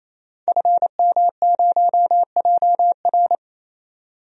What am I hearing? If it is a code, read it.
Morse code, heard as a keyed tone, about 28 wpm, 700 Hz. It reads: FM0JR
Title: DXCC-100 Call Signs - 5 Characters - 28wpm